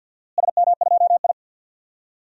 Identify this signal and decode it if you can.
Morse code, heard as a keyed tone, about 52 wpm, 700 Hz. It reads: SM2I
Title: DXCC-100 Call Signs - 4 Characters - Repeated 3x - 50wpm